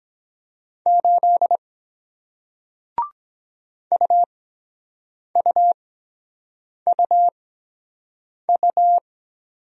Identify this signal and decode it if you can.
Morse code, heard as a keyed tone, about 23 wpm, spaced as for 7 wpm, 700 Hz. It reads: 8 UUUU